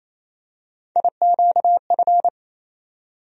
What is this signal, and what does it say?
Morse code, heard as a keyed tone, about 28 wpm, 700 Hz. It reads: IQF
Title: USA Call Sign Postfixes - 3 Letters 28wpm